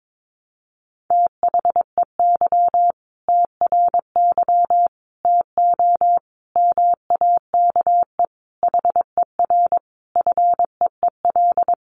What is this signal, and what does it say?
Morse code, heard as a keyed tone, about 22 wpm, 700 Hz. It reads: THEY TRY TO MAKE HER FEEL
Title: Easy Phrases 22wpm